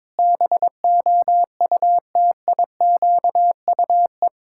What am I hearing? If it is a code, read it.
Morse code, heard as a keyed tone, about 22 wpm, 700 Hz. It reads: BOUTIQUE
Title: Pangram Challenge v2 - 22wpm